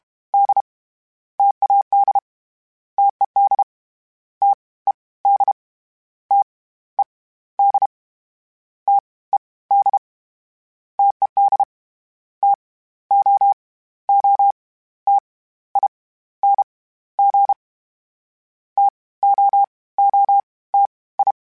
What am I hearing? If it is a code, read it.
Morse code, heard as a keyed tone, about 32 wpm, 800 Hz. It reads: D TAD TED T E D T E D T E D TED T O O T I N G T O O T I
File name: Variable Speed / Variable Speed-mp3s / T AOEI's sp @30